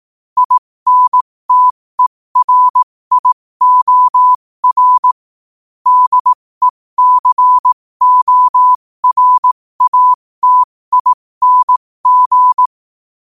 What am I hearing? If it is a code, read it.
Morse code, heard as a keyed tone, about 18 wpm, spaced as for 15 wpm, 1000 Hz. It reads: INTERIOR DECORATING
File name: qotdfiles / 15wpm